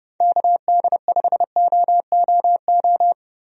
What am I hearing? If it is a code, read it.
Morse code, heard as a keyed tone, about 30 wpm, 700 Hz. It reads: KD5OOO
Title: USA Call Signs - 6 Characters - 30wpm